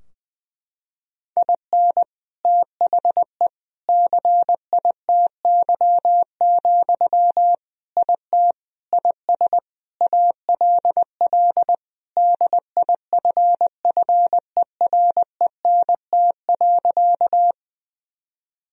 Morse code, heard as a keyed tone, about 19 wpm, 700 Hz. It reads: IN THE CITY, IT IS ALL DIFFERENT.
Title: Sentences from Top 1000 Words - Review Part 3 - 20wpm